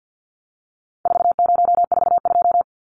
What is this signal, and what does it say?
Morse code, heard as a keyed tone, about 50 wpm, 700 Hz. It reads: 4042
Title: Numbers - 4-Digit 50wpm